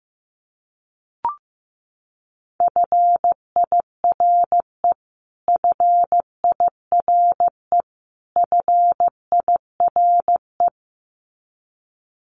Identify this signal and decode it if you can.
Morse code, heard as a keyed tone, about 15 wpm, 700 Hz. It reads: FIRE FIRE FIRE
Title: Top 500 Words Repeated 15wpm